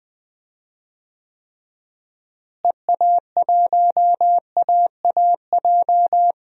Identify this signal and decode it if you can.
Morse code, heard as a keyed tone, about 20 wpm, 700 Hz. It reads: EA1AAJ
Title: DXCC-100 Call Signs - 6 Characters - Repeated 2x - 20wpm